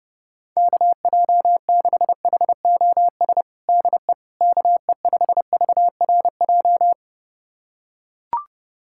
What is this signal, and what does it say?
Morse code, heard as a keyed tone, about 31 wpm, 700 Hz. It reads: KJ6HOS DE KE5VRJ
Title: USA Callsign DE Callsign - 6 Characters - 30wpm